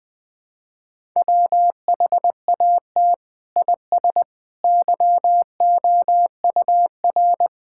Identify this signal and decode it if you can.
Morse code, heard as a keyed tone, about 21 wpm, 700 Hz. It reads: WHAT IS YOUR
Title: Sentences from Top 1000 Words - Review Part 6 - 20wpm